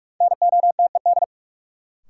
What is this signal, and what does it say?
Morse code, heard as a keyed tone, about 44 wpm, 700 Hz. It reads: NOTED
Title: Sets of 5 Words - New Word Drill - 45wpm